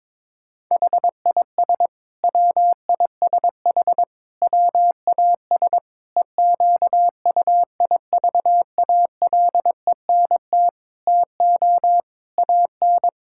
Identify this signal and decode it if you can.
Morse code, heard as a keyed tone, about 22 wpm, 700 Hz. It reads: HIS WISH WAS EQUIVALENT TO AN
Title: Sentences from Top 1600 Words 22wpm